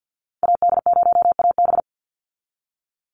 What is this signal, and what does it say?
Morse code, heard as a keyed tone, about 49 wpm, 700 Hz. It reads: AD0AB
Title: Call Signs Repeated 3x - 50wpm